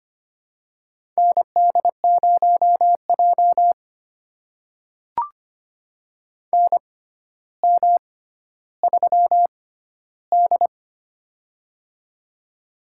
Morse code, heard as a keyed tone, about 25 wpm, spaced as for 13 wpm, 700 Hz. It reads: ND0J NM3D
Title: USA Call Signs - 4 Characters - 6x Character Spacing - 25wpm